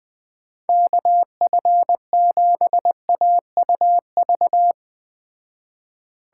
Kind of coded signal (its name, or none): Morse code